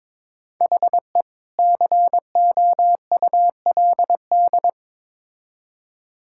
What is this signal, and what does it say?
Morse code, heard as a keyed tone, about 22 wpm, 700 Hz. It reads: HE COULD
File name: Build A Phrase - 2-Word Phrases - 22wpm